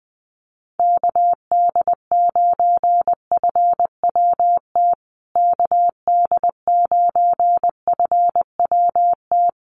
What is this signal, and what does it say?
Morse code, heard as a keyed tone, about 20 wpm, 700 Hz. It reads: KD9FWT KD9FWT
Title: USA Call Signs Repeated 2x 20wpm